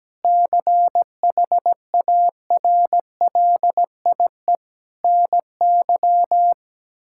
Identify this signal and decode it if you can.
Morse code, heard as a keyed tone, about 17 wpm, 700 Hz. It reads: CHARLIE NY
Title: K1USN SST QSO Practice - 17wpm